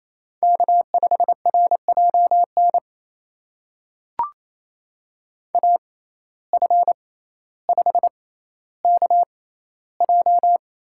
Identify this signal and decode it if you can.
Morse code, heard as a keyed tone, about 28 wpm, spaced as for 14 wpm, 700 Hz. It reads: K5RJN AF5KJ
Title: USA Call Signs - 5 Characters - 6x Character Spacing - 28wpm